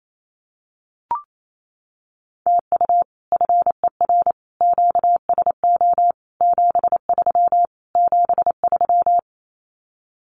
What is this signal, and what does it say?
Morse code, heard as a keyed tone, about 28 wpm, 700 Hz. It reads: TU FER QSO 73 73
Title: Common QSO Phrases Encore 28wpm